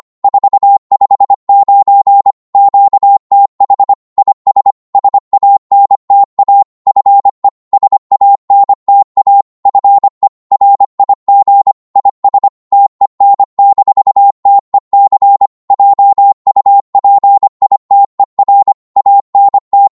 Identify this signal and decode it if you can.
Morse code, heard as a keyed tone, about 25 wpm, 800 Hz. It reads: 459 QTH IS SANTA FE SANTA FE RIG IS TEN-TEC JUPITER ANT
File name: qso-25wpm